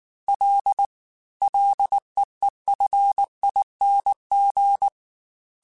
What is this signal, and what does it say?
Morse code, heard as a keyed tone, about 19 wpm, 800 Hz. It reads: L LEEFING
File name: F and L / F and L mp3 / F and L Prefixes 3x2 Twice @18